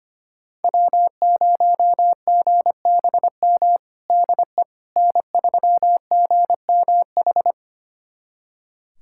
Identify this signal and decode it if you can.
Morse code, heard as a keyed tone, about 25 wpm, 700 Hz. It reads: W0GBM DE N3GMH
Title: USA Callsign DE Callsign - 5 Characters - 25wpm